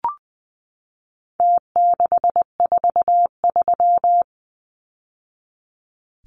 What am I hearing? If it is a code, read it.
Morse code, heard as a keyed tone, about 20 wpm, 700 Hz. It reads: T643
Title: Numbers - Contest Style - 4-Digit 20wpm